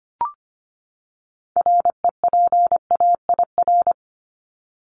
Morse code, heard as a keyed tone, about 25 wpm, 700 Hz. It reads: REPAIR
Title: Top 1501-1750 Words 25wpm